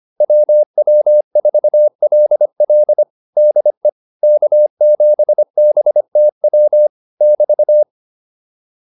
Morse code, heard as a keyed tone, about 25 wpm, 600 Hz. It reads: WW4LL DE K7BTW <BT>